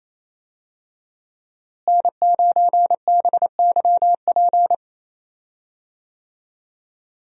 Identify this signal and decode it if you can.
Morse code, heard as a keyed tone, about 28 wpm, 700 Hz. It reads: N9BYP
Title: USA Call Signs - 5 Characters - Repeated 2x - 28wpm